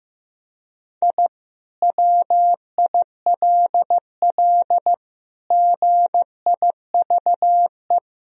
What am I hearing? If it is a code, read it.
Morse code, heard as a keyed tone, about 15 wpm, 700 Hz. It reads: I WILL GIVE
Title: Sentences from Top 1000 Words - Review Part 2 - 15wpm